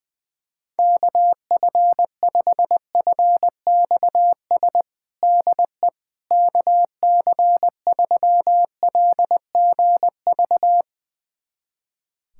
Morse code, heard as a keyed tone, about 20 wpm, 700 Hz. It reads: KF5FXS DE KC3LGV
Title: USA Callsign DE Callsign - 6 Characters - 20wpm